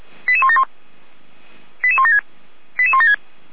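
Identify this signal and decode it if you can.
Five-tone selective call (ZVEI), heard as quick runs of five tones, tones of about 70 ms. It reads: 88171 88176 88176
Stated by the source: ZVEI five-tone call 88174 received by SDR.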